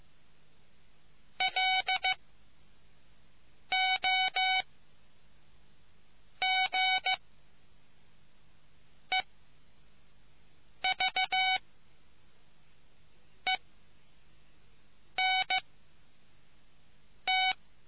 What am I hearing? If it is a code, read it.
Morse code, heard as a keyed tone, about 15 wpm, spaced as for 4 wpm, 750 Hz. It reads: LOG EVENT